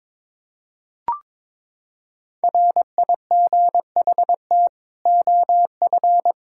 Morse code, heard as a keyed tone, about 22 wpm, 700 Hz. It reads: RIGHT OF
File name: Sets of 9 Words - 22wpm